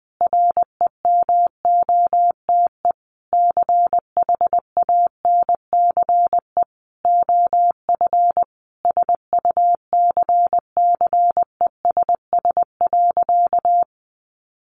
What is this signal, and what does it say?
Morse code, heard as a keyed tone, about 20 wpm, 700 Hz. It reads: REMOTE CHANCE OF SUCCESS.
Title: Sentences from Top 1300 Words 20wpm